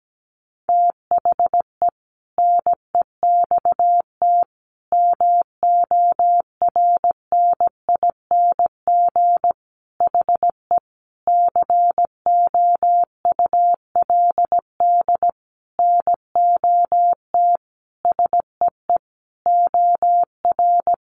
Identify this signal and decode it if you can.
Morse code, heard as a keyed tone, about 17 wpm, 700 Hz. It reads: THE NEXT MORNING HE COULD NOT SEE OR
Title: Sentences from Top 600 Words 17wpm